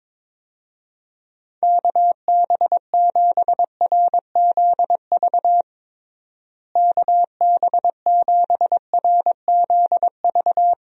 Morse code, heard as a keyed tone, about 22 wpm, 700 Hz. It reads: KB7RZV KB7RZV
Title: USA Call Signs - 6 Characters - Repeated 2x - 22wpm